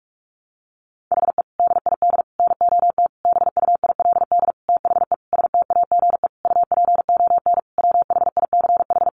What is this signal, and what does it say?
Morse code, heard as a keyed tone, about 44 wpm, 700 Hz. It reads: HE DID NOT BUILD THE STAGE UPON WHICH
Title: Sentences from Top 1200 Words 45wpm